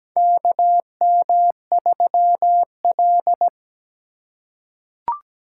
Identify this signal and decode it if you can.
Morse code, heard as a keyed tone, about 17 wpm, 700 Hz. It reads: KM3L
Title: USA Call Signs - 4 Characters - 4x Character Spacing - 17wpm